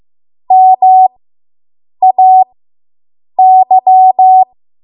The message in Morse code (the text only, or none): MAY